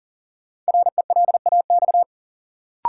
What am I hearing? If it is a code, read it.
Morse code, heard as a keyed tone, about 40 wpm, 700 Hz. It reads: RELAX
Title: Sentences from Top 1400 Words - New Word Drill - 40wpm